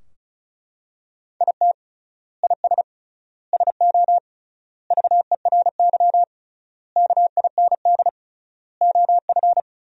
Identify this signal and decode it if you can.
Morse code, heard as a keyed tone, about 36 wpm, 700 Hz. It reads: IT IS SO VERY KIND OF
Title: 3x Word-Spacing - Sentences from Top 300 Words - 35wpm